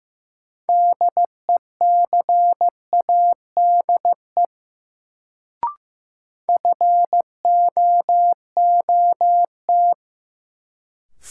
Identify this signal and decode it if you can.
Morse code, heard as a keyed tone, about 15 wpm, 700 Hz. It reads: DECADE FOOT
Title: Sets of 5 Words - New Word Drill - 15wpm